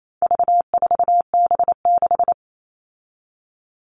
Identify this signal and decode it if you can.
Morse code, heard as a keyed tone, about 28 wpm, 700 Hz. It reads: V4B6
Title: All Cut Numbers - 4-Digit 28wpm